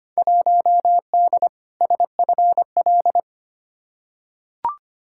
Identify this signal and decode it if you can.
Morse code, heard as a keyed tone, about 25 wpm, 700 Hz. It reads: 1D SFL
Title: Field Day Exchanges - 25wpm